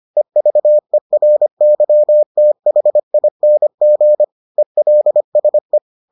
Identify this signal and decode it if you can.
Morse code, heard as a keyed tone, about 25 wpm, 600 Hz. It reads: EVERYTHING ELSE